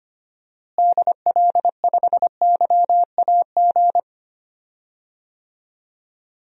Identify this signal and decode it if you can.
Morse code, heard as a keyed tone, about 25 wpm, 700 Hz. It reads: DL5YAG
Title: DXCC-100 Call Signs - 6 Characters - Repeated 3x - 25wpm